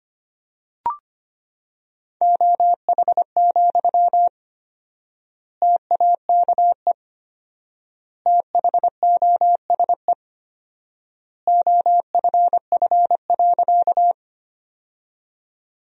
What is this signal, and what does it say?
Morse code, heard as a keyed tone, about 25 wpm, 700 Hz. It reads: OH, TAKE THOSE OFF.
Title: 4x Word-Spacing - Sentences from Top 500 Words - 25wpm